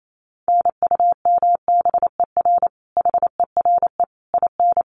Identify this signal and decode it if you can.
Morse code, heard as a keyed tone, about 28 wpm, 700 Hz. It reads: NUMBER HERE IN
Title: Sets of 5 Words Encore2 28wpm